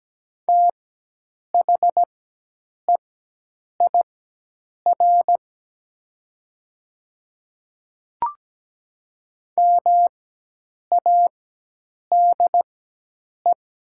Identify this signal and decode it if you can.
Morse code, heard as a keyed tone, about 17 wpm, spaced as for 8 wpm, 700 Hz. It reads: THEIR MADE
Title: 4x Character Spacing - Top 200 Words - 17wpm